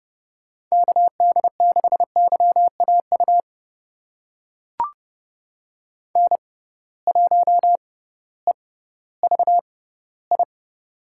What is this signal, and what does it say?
Morse code, heard as a keyed tone, about 30 wpm, spaced as for 16 wpm, 700 Hz. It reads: KD6YAU N1EVI
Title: USA Call Signs - 6x Character Spacing - 30wpm